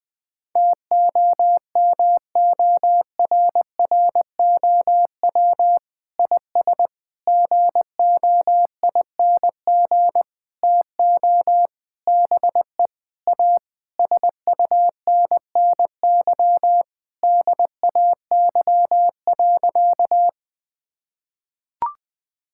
Morse code, heard as a keyed tone, about 20 wpm, 700 Hz. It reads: TOMORROW IS GOING TO BE A SUNNY DAY.
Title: Sentences with 5-Word Phrases - 20wpm